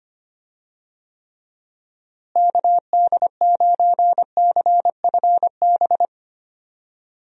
Morse code, heard as a keyed tone, about 25 wpm, 700 Hz. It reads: KD9CFB
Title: USA Call Signs - 6 Characters - Repeated 2x - 25wpm